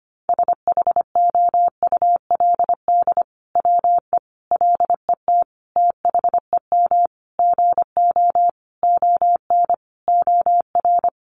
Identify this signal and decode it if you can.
Morse code, heard as a keyed tone, about 25 wpm, 700 Hz. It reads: SHOULD WE LET THEM GO ON OR
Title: Sentences from Top 600 Words 25wpm